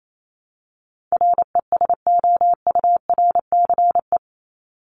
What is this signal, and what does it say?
Morse code, heard as a keyed tone, about 28 wpm, 700 Hz. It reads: RESOURCE
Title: Top 1001-1250 Words 28wpm